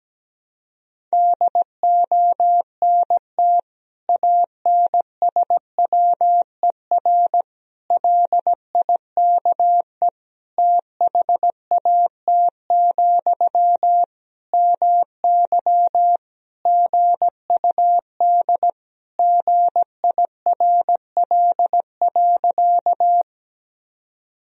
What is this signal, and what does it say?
Morse code, heard as a keyed tone, about 17 wpm, 700 Hz. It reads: DONT ANSWER LIKE THAT, MY GUD GIRL.
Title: Sentences from Top 400 Words and Abbreviations 17wpm